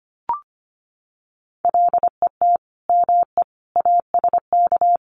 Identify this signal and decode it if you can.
Morse code, heard as a keyed tone, about 25 wpm, 700 Hz. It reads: LET ME ASK
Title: Sets of 5 Words 25wpm